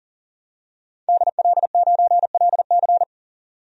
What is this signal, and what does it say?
Morse code, heard as a keyed tone, about 40 wpm, 700 Hz. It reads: DL9LC
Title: DXCC-100 Call Signs - 5 Characters - 40wpm